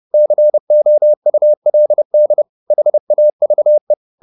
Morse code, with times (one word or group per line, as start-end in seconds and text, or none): 0.1-2.4 COULD
2.7-3.9 HAVE